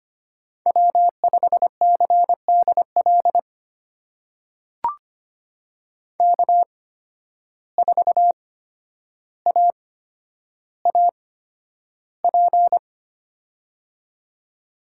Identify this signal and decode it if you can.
Morse code, heard as a keyed tone, about 25 wpm, spaced as for 11 wpm, 700 Hz. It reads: W5CDL K4AAP
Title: USA Call Signs - 5 Characters - 8x Character Spacing - 25wpm